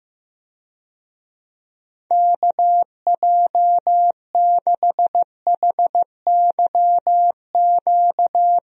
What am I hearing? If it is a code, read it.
Morse code, heard as a keyed tone, about 15 wpm, 700 Hz. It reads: KJ6HYQ
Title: USA Call Signs - 6 Characters - Repeated 2x - 15wpm